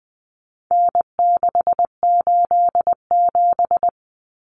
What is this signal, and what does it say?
Morse code, heard as a keyed tone, about 20 wpm, 700 Hz. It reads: N687
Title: Common Cut Numbers - 4-Digit 20wpm